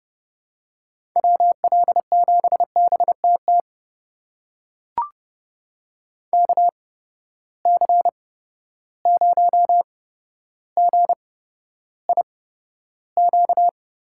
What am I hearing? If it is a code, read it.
Morse code, heard as a keyed tone, about 30 wpm, spaced as for 13 wpm, 700 Hz. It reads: WL7BTT KC0GIQ
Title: USA Call Signs - 6 Characters - 8x Character Spacing - 30wpm